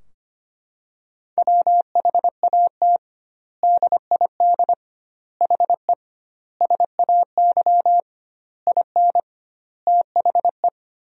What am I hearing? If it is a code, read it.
Morse code, heard as a keyed tone, about 25 wpm, 700 Hz. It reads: WHAT DID HE SAY IN THE
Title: 2x Word-Spacing - Sentences from Top 300 Words - 25wpm